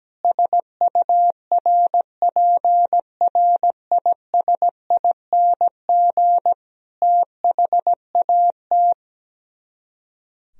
Morse code, heard as a keyed tone, about 17 wpm, 700 Hz. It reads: SURPRISING THAT
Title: Speed-Racing - Sets of 5 Words 17wpm